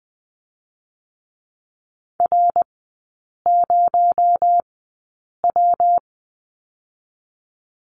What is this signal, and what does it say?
Morse code, heard as a keyed tone, about 20 wpm, spaced as for 8 wpm, 700 Hz. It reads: R0W
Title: 218 Lesson 35e - Number 0 - Three Characters 20wpm